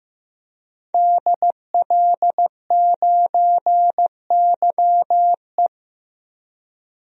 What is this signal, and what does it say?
Morse code, heard as a keyed tone, about 16 wpm, 700 Hz. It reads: DL9YE
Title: DXCC-100 Call Signs - 5 Characters - 15wpm